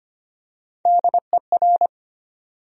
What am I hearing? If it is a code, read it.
Morse code, heard as a keyed tone, about 25 wpm, 700 Hz. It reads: DER
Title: USA Call Sign Postfixes - 3 Letters 25wpm